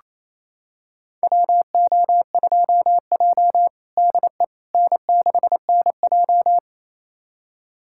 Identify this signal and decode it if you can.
Morse code, heard as a keyed tone, about 28 wpm, 700 Hz. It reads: WO2J DE N6NJ
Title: USA Callsign DE Callsign - 4 Characters - 28wpm